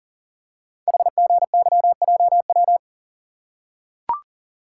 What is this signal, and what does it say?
Morse code, heard as a keyed tone, about 42 wpm, 700 Hz. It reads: HGYJW
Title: Code Groups - 5 Letters - 40wpm